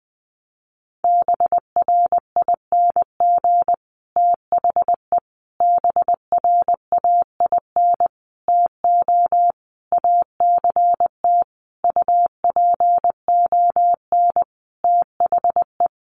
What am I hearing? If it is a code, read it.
Morse code, heard as a keyed tone, about 20 wpm, 700 Hz. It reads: BRING THE BRAIN TO ACT UPON THE
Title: Sentences from Top 1200 Words 20wpm